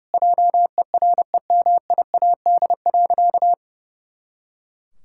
Morse code, heard as a keyed tone, about 30 wpm, 700 Hz. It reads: JEREMIAD.
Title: Pangram Challenge v2 - 30wpm